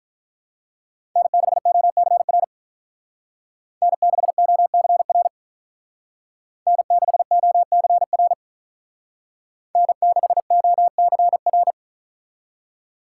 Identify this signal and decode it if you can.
Morse code, heard as a keyed tone, about 45 wpm, 700 Hz. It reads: N6OCR N6OCR N6OCR N6OCR